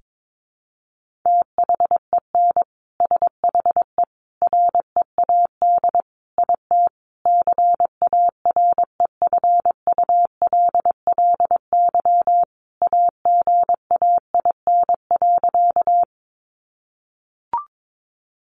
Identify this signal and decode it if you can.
Morse code, heard as a keyed tone, about 22 wpm, 700 Hz. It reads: THEN SHE READ IT CAREFULLY AGAIN.
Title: Sentences from Top 1300 Words 22wpm